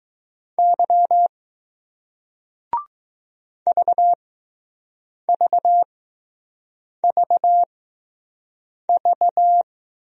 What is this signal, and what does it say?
Morse code, heard as a keyed tone, about 20 wpm, spaced as for 6 wpm, 700 Hz. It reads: Y VVVV